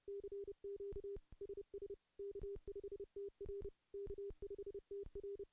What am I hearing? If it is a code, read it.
Morse code, heard as a keyed tone, about 30 wpm, 400 Hz. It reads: CQ SS K5TR K5TR
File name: CW-SS-k5tr